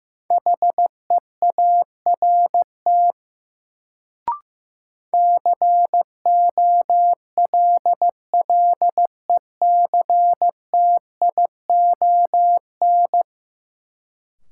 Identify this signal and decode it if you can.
Morse code, heard as a keyed tone, about 15 wpm, 700 Hz. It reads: HEART COLLECTION
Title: Sentences from Top 1200 Words - New Word Drill - 15wpm